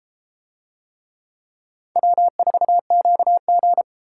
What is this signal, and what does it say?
Morse code, heard as a keyed tone, about 30 wpm, 700 Hz. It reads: W4QG